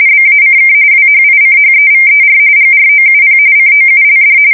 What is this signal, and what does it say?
RTTY, about 45.45 baud. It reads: BACK WHICHEVER GROUP COULD
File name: fsk170 / USS-New-Jersey